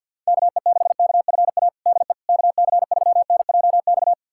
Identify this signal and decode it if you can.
Morse code, heard as a keyed tone, about 52 wpm, 700 Hz. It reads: KE6KFA DE KC3NJX
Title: USA Callsign DE Callsign - 6 Characters - 50wpm